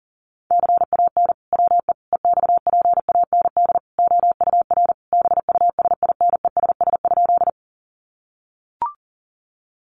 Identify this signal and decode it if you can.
Morse code, heard as a keyed tone, about 40 wpm, 700 Hz. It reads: CAN WE EXPAND OUR BUSINESS?
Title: Sentences from Top 1400 Words 40wpm